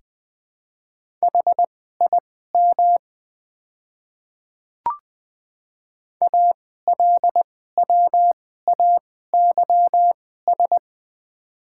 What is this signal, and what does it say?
Morse code, heard as a keyed tone, about 20 wpm, spaced as for 14 wpm, 700 Hz. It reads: HIM ALWAYS
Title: 2x Character Spacing - Top 300 Words - 20wpm